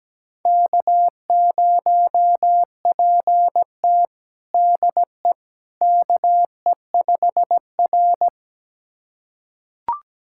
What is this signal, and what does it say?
Morse code, heard as a keyed tone, about 17 wpm, 700 Hz. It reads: K0PT DE KE5R
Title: USA Callsign DE Callsign - 4 Characters - 17wpm